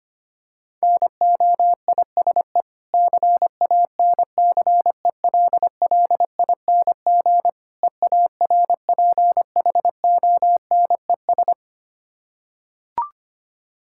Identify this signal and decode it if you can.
Morse code, heard as a keyed tone, about 25 wpm, 700 Hz. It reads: NOISE CANCELLING EARPHONES
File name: practice / 25wpm / Difficult Words with Hints - 25wpm